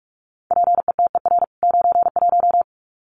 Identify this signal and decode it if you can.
Morse code, heard as a keyed tone, about 44 wpm, 700 Hz. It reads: PETER 91
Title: CWT Exchange 45wpm